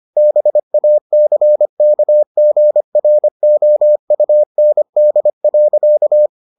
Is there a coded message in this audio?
Yes: Morse code from 0.2 to 6.3 s, heard as a keyed tone, about 25 wpm, 600 Hz. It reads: BACKGROUND.